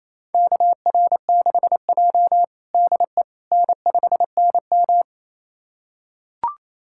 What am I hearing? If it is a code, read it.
Morse code, heard as a keyed tone, about 28 wpm, 700 Hz. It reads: KR6J DE N5NM
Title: USA Callsign DE Callsign - 4 Characters - 28wpm